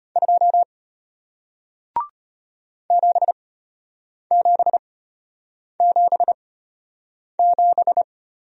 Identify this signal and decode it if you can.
Morse code, heard as a keyed tone, about 33 wpm, spaced as for 8 wpm, 700 Hz. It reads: 2 7777